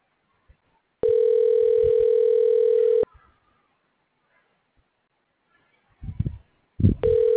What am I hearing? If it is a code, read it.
Ringback tone (North America), 440 + 480 Hz.